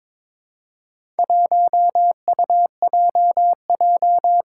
Morse code, heard as a keyed tone, about 23 wpm, 700 Hz. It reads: 1UJJ
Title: Four Challenging Letter-Number Combinations 22wpm